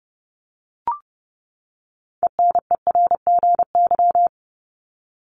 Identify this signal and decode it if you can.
Morse code, heard as a keyed tone, about 30 wpm, 700 Hz. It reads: ENERGY
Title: Top 751-1000 Words 30wpm